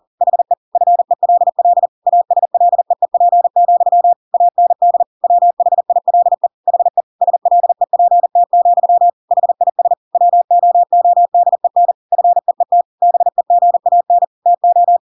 Morse code, heard as a keyed tone, about 40 wpm, 700 Hz. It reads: HE FELL ASLEEP, AND WHILE HE SLEPT, HIS WOODEN FEET BEGAN TO
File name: Pinocchio / Pinocchio ch6 40 40 ik1hge